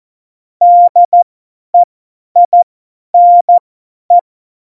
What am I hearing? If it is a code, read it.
Morse code, heard as a keyed tone, about 14 wpm, spaced as for 10 wpm, 700 Hz. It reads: DEINE
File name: Morsekurs / Witze / Witz 03. November 2020, 50 F70